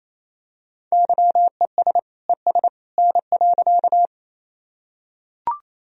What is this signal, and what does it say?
Morse code, heard as a keyed tone, about 28 wpm, 700 Hz. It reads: YES ES N.
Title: Sentences from Top 500 Words and Abbreviations 28wpm